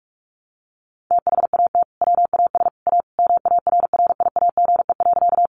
Morse code, heard as a keyed tone, about 44 wpm, 700 Hz. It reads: THAT WAS A MARRIAGE.
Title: Sentences from Top 800 Words 45wpm